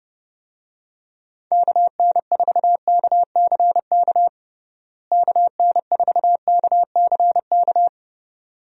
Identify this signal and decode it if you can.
Morse code, heard as a keyed tone, about 30 wpm, 700 Hz. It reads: KN4KCK KN4KCK
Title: USA Call Signs - 6 Characters - Repeated 2x - 30wpm